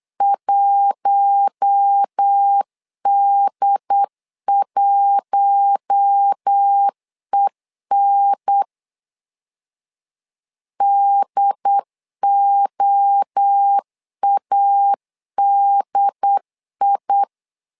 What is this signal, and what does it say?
Morse code, heard as a keyed tone, about 8 wpm, 800 Hz. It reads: 1D1EN DOADI
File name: LEZIONI / 4 Q1 8wpm 3035